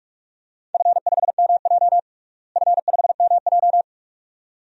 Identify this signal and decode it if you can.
Morse code, heard as a keyed tone, about 46 wpm, 700 Hz. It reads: F5MJ F5MJ
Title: DXCC-100 Call Signs - 4 Characters - Repeated 2x - 45wpm